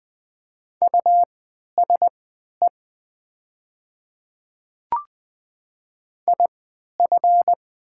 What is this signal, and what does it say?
Morse code, heard as a keyed tone, about 20 wpm, spaced as for 11 wpm, 700 Hz. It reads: USE IF
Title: 3x Character Spacing - Top 100 Words - 20wpm